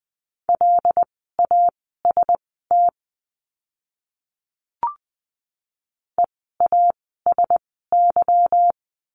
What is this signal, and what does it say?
Morse code, heard as a keyed tone, about 20 wpm, spaced as for 15 wpm, 700 Hz. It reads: LAST EASY
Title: 146 Lesson 24f - Letter y - Words 20wpm